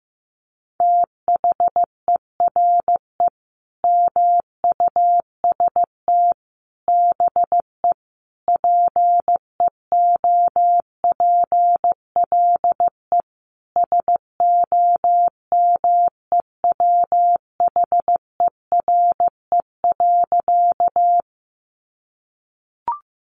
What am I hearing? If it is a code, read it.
Morse code, heard as a keyed tone, about 15 wpm, 700 Hz. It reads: THERE MUST BE PEOPLE SOMEWHERE.
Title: Sentences from Top 1400 Words 15wpm